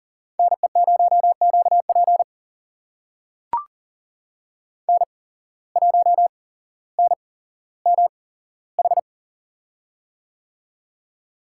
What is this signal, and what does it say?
Morse code, heard as a keyed tone, about 40 wpm, spaced as for 17 wpm, 700 Hz. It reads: NE0QP N1NMH
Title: USA Call Signs - 5 Characters - 8x Character Spacing - 40wpm